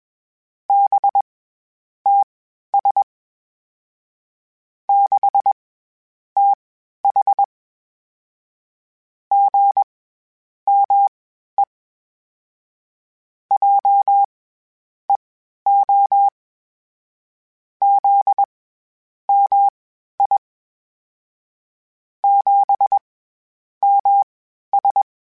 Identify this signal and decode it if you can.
Morse code, heard as a keyed tone, about 20 wpm, spaced as for 10 wpm, 800 Hz. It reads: BTS 6TH GME JEO ZMI 7MS
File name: CW Comparisons / CW Comparisons-mp3s / CW Comparisons Broken Characters @20